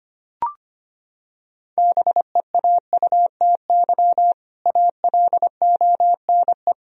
Morse code, heard as a keyed tone, about 25 wpm, 700 Hz. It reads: BEAUTY ALONE
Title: Sentences from Top 1000 Words - Review Part 2 - 25wpm